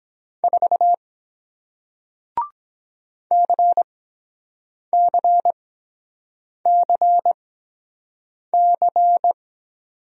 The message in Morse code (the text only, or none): 4 CCCC